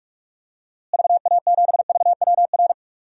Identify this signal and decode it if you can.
Morse code, heard as a keyed tone, about 42 wpm, 700 Hz. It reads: VA7VWR
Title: DXCC-100 Call Signs - 6 Characters - Repeated 2x - 45wpm